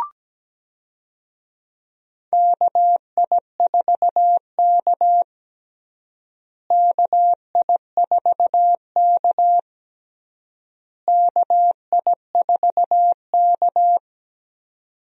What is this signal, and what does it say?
Morse code, heard as a keyed tone, about 17 wpm, 700 Hz. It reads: KI4K KI4K KI4K
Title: USA Call Signs - 4 Characters - Repeated 3x - 17wpm